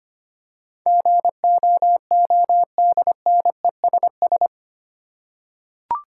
Morse code, heard as a keyed tone, about 25 wpm, 700 Hz. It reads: GOODNESS
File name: 8 Letter Words - First 3 Letters Given - 25wpm